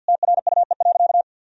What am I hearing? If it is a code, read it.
Morse code, heard as a keyed tone, about 50 wpm, 700 Hz. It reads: TRUE.
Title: Sentences from Top 1000 Words - Review Part 3 - 50wpm